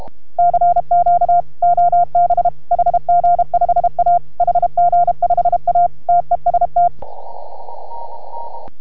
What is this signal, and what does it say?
Morse code, heard as a keyed tone, about 33 wpm, 690 Hz. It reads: CQ OB HG5A HG5A TEST